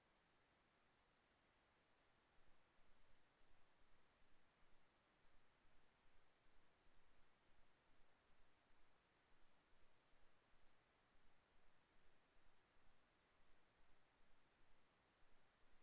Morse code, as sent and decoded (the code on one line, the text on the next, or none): none